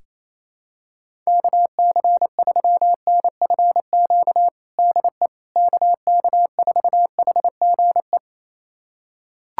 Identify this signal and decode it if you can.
Morse code, heard as a keyed tone, about 28 wpm, 700 Hz. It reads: KC3NFQ DE KK4HGE
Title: USA Callsign DE Callsign - 6 Characters - 28wpm